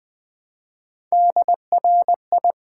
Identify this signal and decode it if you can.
Morse code, heard as a keyed tone, about 20 wpm, 700 Hz. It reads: DRI
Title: Build A Word - 1 Letter At A Time - 5-Letter Words - 20wpm